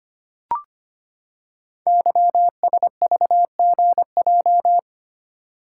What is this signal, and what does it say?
Morse code, heard as a keyed tone, about 25 wpm, 700 Hz. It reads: YSVGJ
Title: Code Groups - 5 Letters - 25wpm